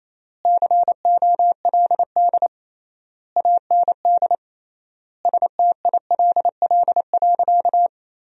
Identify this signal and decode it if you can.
Morse code, heard as a keyed tone, about 28 wpm, 700 Hz. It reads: COLD AND STILL.
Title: 3x Word-Spacing - Sentences from Top 400 Words - 28wpm